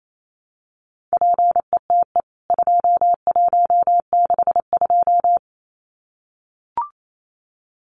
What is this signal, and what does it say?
Morse code, heard as a keyed tone, about 28 wpm, 700 Hz. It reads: PETE 2162
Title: CWT Exchange 28wpm